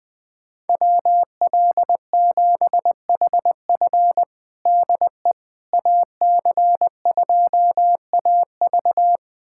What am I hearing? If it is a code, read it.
Morse code, heard as a keyed tone, about 20 wpm, 700 Hz. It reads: WL7HF DE AC2AV
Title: USA Callsign DE Callsign - 5 Characters - 20wpm